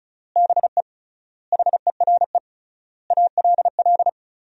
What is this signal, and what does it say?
Morse code, heard as a keyed tone, about 36 wpm, 700 Hz. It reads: BE HERE ALL
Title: 3x Word-Spacing - Sentences from Top 300 Words - 35wpm